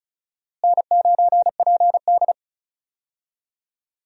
Morse code, heard as a keyed tone, about 36 wpm, 700 Hz. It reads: N9PD
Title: USA Call Signs - 4 Characters - Repeated 3x - 35wpm